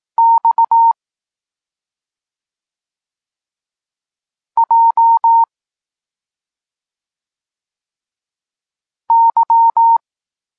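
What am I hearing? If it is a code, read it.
Morse code, heard as a keyed tone, about 18 wpm, 920 Hz. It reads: X J Y
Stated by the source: Polish Morse lesson teaching the letter C, from SP9MOA Niepołomicki Klub Krótkofalowców.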